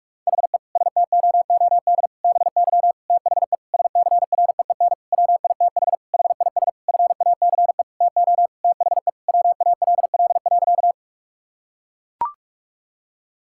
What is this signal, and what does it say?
Morse code, heard as a keyed tone, about 41 wpm, 700 Hz. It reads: HE STOOD BY THE SCREEN WITH HIS FACE TO THE WALL.
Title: Sentences from Top 1000 Words - Review Part 6 - 45wpm